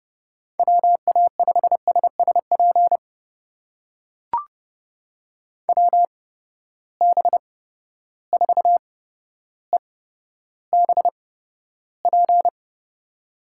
Morse code, heard as a keyed tone, about 30 wpm, spaced as for 13 wpm, 700 Hz. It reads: WA5SSP WB4EBP
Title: USA Call Signs - 6 Characters - 8x Character Spacing - 30wpm